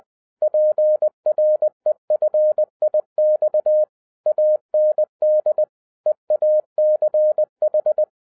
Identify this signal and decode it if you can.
Morse code, heard as a keyed tone, about 20 wpm, 610 Hz. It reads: PREFIX AND EACH